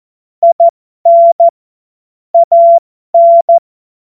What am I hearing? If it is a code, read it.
Morse code, heard as a keyed tone, about 14 wpm, spaced as for 12 wpm, 680 Hz. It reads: IN AN
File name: Morsekurs / Witze